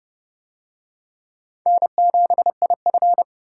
Morse code, heard as a keyed tone, about 25 wpm, 700 Hz. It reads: N7IF
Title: Speed Racing - USA 1x2 Call Signs 20wpm